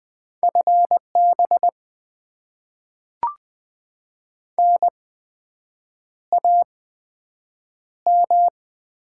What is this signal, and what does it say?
Morse code, heard as a keyed tone, about 20 wpm, 700 Hz. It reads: F B N A M
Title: Top 20 QSO Elements - 8x Character Spacing - 20wpm